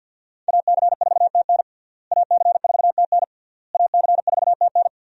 Morse code, heard as a keyed tone, about 52 wpm, 700 Hz. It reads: AC4TN AC4TN AC4TN
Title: USA Call Signs - 5 Characters - Repeated 3x - 50wpm